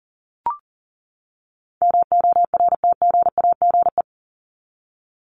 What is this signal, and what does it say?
Morse code, heard as a keyed tone, about 38 wpm, 700 Hz. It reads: MORTGAGE
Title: Top 1751-2000 Words 40wpm